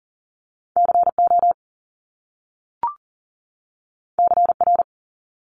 Morse code, heard as a keyed tone, about 40 wpm, 700 Hz. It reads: CO CR
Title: Top Two Letter Combinations in Words 40wpm